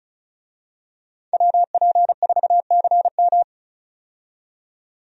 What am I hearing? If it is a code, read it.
Morse code, heard as a keyed tone, about 36 wpm, 700 Hz. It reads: WP4CM
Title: USA Call Signs - 5 Characters - 35wpm